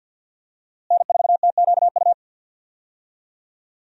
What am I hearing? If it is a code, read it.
Morse code, heard as a keyed tone, about 50 wpm, 700 Hz. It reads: N4TQU
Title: USA Call Signs - 5 Characters - Repeated 2x - 50wpm